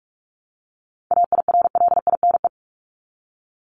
Morse code, heard as a keyed tone, about 44 wpm, 700 Hz. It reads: AIRLINE
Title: Top 1501-1750 Words 45wpm